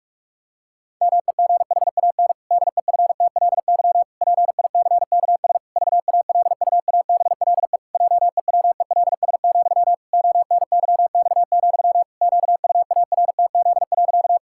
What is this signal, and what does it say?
Morse code, heard as a keyed tone, about 46 wpm, 700 Hz. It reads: MEGHAN DEFTLY PICKS VALUABLE JEWELS, ONYX, QUARTZ.
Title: Pangram Challenge v2 - 45wpm